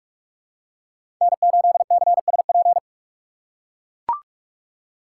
Morse code, heard as a keyed tone, about 44 wpm, 700 Hz. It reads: N8CSP
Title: USA Call Signs - 5 Characters - 6x Character Spacing - 45wpm